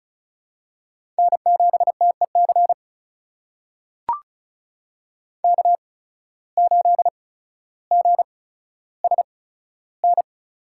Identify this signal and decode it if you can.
Morse code, heard as a keyed tone, about 36 wpm, spaced as for 15 wpm, 700 Hz. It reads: N7TEC K8GSN
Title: USA Call Signs - 5 Characters - 8x Character Spacing - 35wpm